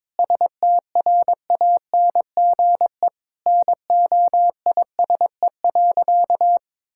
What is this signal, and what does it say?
Morse code, heard as a keyed tone, about 22 wpm, 700 Hz. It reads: STRANGE NOISE.
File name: practice / 22wpm / Sentences with 8-Word Phrases - 22wpm